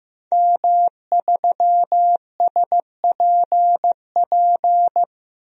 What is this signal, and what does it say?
Morse code, heard as a keyed tone, about 15 wpm, 700 Hz. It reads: M3SPP
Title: DXCC-100 Call Signs - 5 Characters - 15wpm